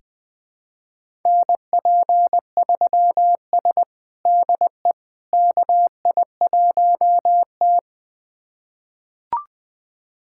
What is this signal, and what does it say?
Morse code, heard as a keyed tone, about 20 wpm, 700 Hz. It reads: NP3S DE KI1T
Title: USA Callsign DE Callsign - 4 Characters - 20wpm